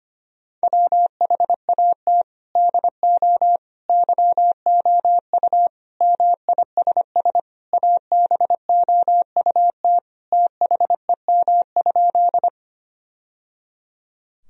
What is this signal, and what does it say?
Morse code, heard as a keyed tone, about 25 wpm, 700 Hz. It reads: WHAT DO YOU MISS ABOUT THEM?
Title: Sentences from Top 1000 Words - Review Part 3 - 25wpm